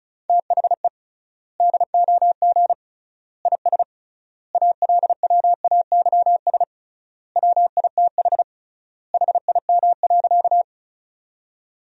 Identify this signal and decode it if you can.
Morse code, heard as a keyed tone, about 36 wpm, 700 Hz. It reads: THE DOG IS ALWAYS WITH HIM.
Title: 3x Word-Spacing - Sentences from Top 400 Words - 35wpm